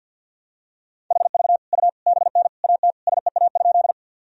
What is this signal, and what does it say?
Morse code, heard as a keyed tone, about 46 wpm, 700 Hz. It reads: HV U BN AT SEA?
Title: Sentences from Top 300 Words and Abbreviations 50wpm